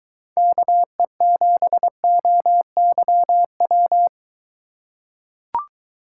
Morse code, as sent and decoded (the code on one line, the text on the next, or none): -.- . --... --- -.-- .--
KE7OYW